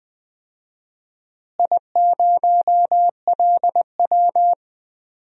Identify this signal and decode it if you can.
Morse code, heard as a keyed tone, about 20 wpm, 700 Hz. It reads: I0LW
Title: DXCC-100 Call Signs - 4 Characters - Repeated 3x - 20wpm